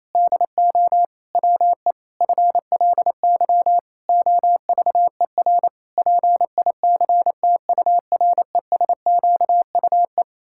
Morse code, heard as a keyed tone, about 28 wpm, 700 Hz. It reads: DO WE FLY OVER PICTURESQUE
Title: Pangram Challenge v2 - 28wpm